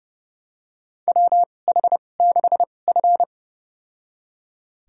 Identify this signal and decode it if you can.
Morse code, heard as a keyed tone, about 30 wpm, spaced as for 22 wpm, 700 Hz. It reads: WH6F
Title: USA Call Signs - 4 Characters - 2x Character Spacing - 30wpm